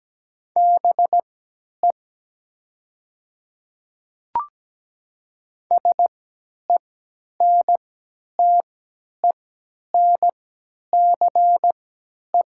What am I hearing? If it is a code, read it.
Morse code, heard as a keyed tone, about 17 wpm, spaced as for 10 wpm, 700 Hz. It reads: BE SENTENCE
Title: 3x Character Spacing - Top 200 Words - 17wpm